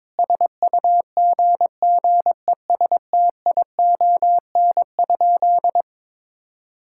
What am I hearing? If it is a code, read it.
Morse code, heard as a keyed tone, about 22 wpm, 700 Hz. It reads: SUGGESTION?
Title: Sentences from Top 1600 Words 22wpm